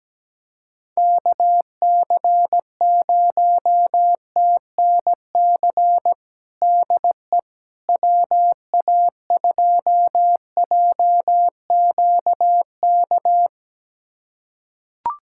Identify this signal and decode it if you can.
Morse code, heard as a keyed tone, about 17 wpm, 700 Hz. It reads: KC0TNC DE WA2JQK
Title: USA Callsign DE Callsign - 6 Characters - 17wpm